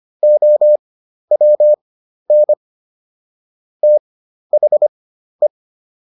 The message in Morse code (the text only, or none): OWN THE